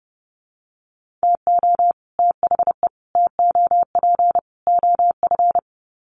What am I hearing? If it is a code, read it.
Morse code, heard as a keyed tone, about 30 wpm, 700 Hz. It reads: TO THE TOP OF
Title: Sets of 4 Words 30wpm